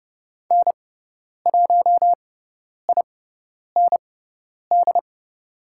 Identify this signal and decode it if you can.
Morse code, heard as a keyed tone, about 30 wpm, spaced as for 10 wpm, 700 Hz. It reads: N1IND
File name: Lesson 16g - Letter c - Call Signs 30wpm